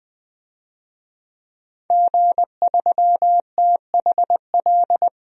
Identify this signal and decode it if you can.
Morse code, heard as a keyed tone, about 20 wpm, 700 Hz. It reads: G3THL
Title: DXCC-100 Call Signs - 5 Characters - 20wpm